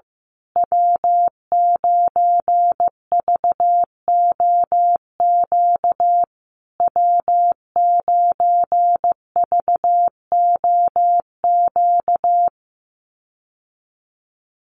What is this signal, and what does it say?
Morse code, heard as a keyed tone, about 15 wpm, 700 Hz. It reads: W9VOQ W9VOQ
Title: USA Call Signs Repeated 3x 15wpm